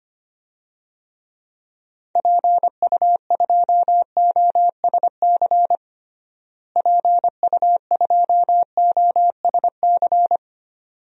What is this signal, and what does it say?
Morse code, heard as a keyed tone, about 25 wpm, 700 Hz. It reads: PU2OSC PU2OSC
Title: DXCC-100 Call Signs - 6 Characters - Repeated 2x - 25wpm